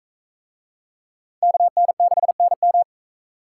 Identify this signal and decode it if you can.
Morse code, heard as a keyed tone, about 38 wpm, 700 Hz. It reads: KN6NM